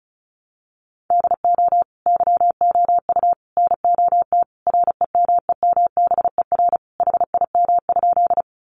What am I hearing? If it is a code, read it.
Morse code, heard as a keyed tone, about 34 wpm, 700 Hz. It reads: DO YOU NOT REMEMBER HIM?
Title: Sentences from Top 600 Words 35wpm